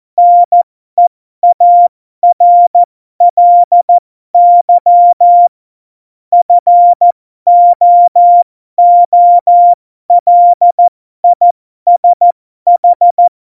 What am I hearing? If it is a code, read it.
Morse code, heard as a keyed tone, about 14 wpm, spaced as for 12 wpm, 700 Hz. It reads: NEARLY FOOLISH
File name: Morsekurs / Witze